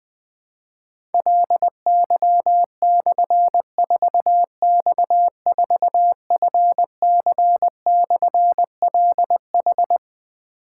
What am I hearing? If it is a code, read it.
Morse code, heard as a keyed tone, about 20 wpm, 700 Hz. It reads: LY/4X4FC/LH
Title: Difficult Call Signs 20wpm